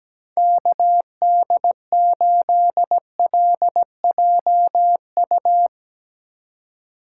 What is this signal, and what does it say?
Morse code, heard as a keyed tone, about 17 wpm, 700 Hz. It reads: KD8LJU
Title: USA Call Signs - 6 Characters - 3x Character Spacing - 17wpm